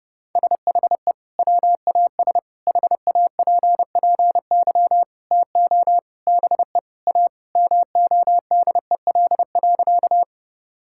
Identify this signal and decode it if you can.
Morse code, heard as a keyed tone, about 30 wpm, 700 Hz. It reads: SHE WAS HAPPY TO BE A MODEL.
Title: Sentences from Top 1000 Words - Review Part 5 - 30wpm